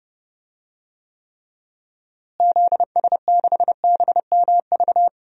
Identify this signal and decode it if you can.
Morse code, heard as a keyed tone, about 31 wpm, 700 Hz. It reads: ZS6BMV
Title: DXCC-100 Call Signs - 6 Characters - Repeated 3x - 30wpm